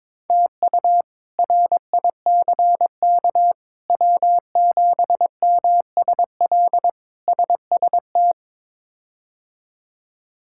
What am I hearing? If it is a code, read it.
Morse code, heard as a keyed tone, about 22 wpm, 700 Hz. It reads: TU RICK W7MSL SST
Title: K1USN SST QSO Practice - 22wpm